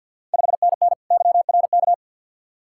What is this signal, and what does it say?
Morse code, heard as a keyed tone, about 53 wpm, 690 Hz. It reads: 5NN YRK
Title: Ontario QSO Party Exchange 50wpm